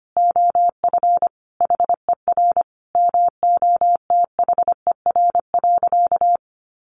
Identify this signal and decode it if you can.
Morse code, heard as a keyed tone, about 25 wpm, 700 Hz. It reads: OF HER MOTHER.
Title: Sentences from Top 1100 Words 25wpm